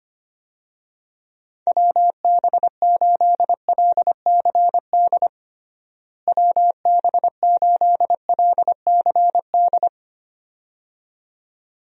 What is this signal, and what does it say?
Morse code, heard as a keyed tone, about 25 wpm, 700 Hz. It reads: WB8LCD WB8LCD
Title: USA Call Signs - 6 Characters - Repeated 2x - 25wpm